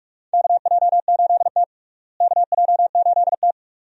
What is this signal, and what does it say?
Morse code, heard as a keyed tone, about 46 wpm, 700 Hz. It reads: KJ8T KJ8T
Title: USA Call Signs - 4 Characters - Repeated 2x - 45wpm